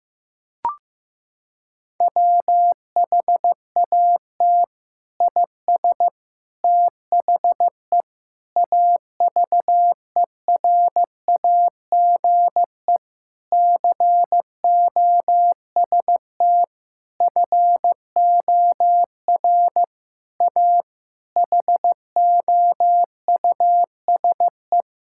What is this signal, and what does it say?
Morse code, heard as a keyed tone, about 15 wpm, 700 Hz. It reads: WHAT IS THE AVERAGE COST FOR A HOUSE
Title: Sentences from Top 1000 Words - Review Part 6 - 15wpm